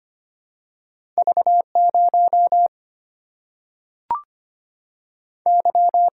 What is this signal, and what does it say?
Morse code, heard as a keyed tone, about 25 wpm, 700 Hz. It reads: V0 Y
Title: Single-Letter Single-Number 25wpm